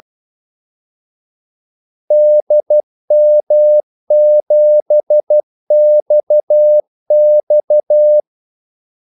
Morse code, heard as a keyed tone, about 12 wpm, 600 Hz. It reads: DM7XX
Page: 2 ekstra ordmellemrum mellem kaldesignalerne